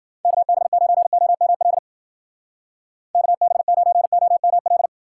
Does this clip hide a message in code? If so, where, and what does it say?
Yes: Morse code from 0.2 to 4.9 s, heard as a keyed tone, about 52 wpm, 700 Hz. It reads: KB9OML KB9OML